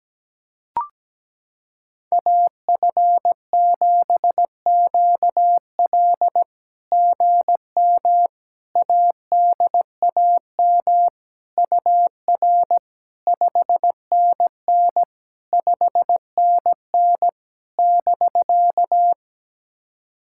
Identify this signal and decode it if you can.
Morse code, heard as a keyed tone, about 17 wpm, 700 Hz. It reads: AF7QL GM ADAM UR 5NN 5NN <BK>
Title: SOTA QSO Practice 17wpm